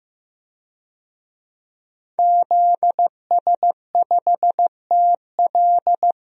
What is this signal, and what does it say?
Morse code, heard as a keyed tone, about 15 wpm, 700 Hz. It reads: ZS5TL
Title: DXCC-100 Call Signs - 5 Characters - 15wpm